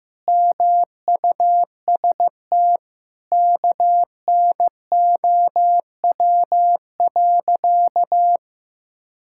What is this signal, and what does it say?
Morse code, heard as a keyed tone, about 14 wpm, 700 Hz. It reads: MUST KNOW.
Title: Sentences from Top 1000 Words - Review Part 6 - 15wpm